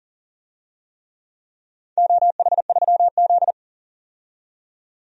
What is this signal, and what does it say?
Morse code, heard as a keyed tone, about 38 wpm, 700 Hz. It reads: OH3Z
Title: DXCC-100 Call Signs - 4 Characters - 40wpm